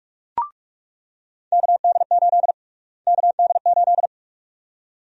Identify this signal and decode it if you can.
Morse code, heard as a keyed tone, about 44 wpm, 700 Hz. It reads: KD8 KD8
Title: USA Call Sign Prefixes - Repeated 2x - 45wpm